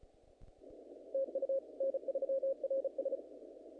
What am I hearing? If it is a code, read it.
Morse code, heard as a keyed tone, about 39 wpm, 530 Hz. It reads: TU N3RS
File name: contests / naqp-cw-k5tr-op-k5pi- -2000